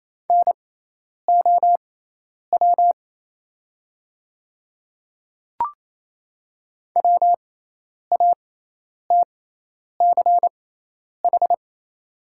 Morse code, heard as a keyed tone, about 27 wpm, spaced as for 10 wpm, 690 Hz. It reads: NOW WATCH
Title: 6x Character Spacing - Top 500 Words - 28wpm